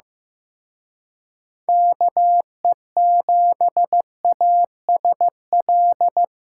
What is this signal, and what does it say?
Morse code, heard as a keyed tone, about 18 wpm, 700 Hz. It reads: KE7ASL
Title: Speed Racing - USA 2x3 Call Signs 15wpm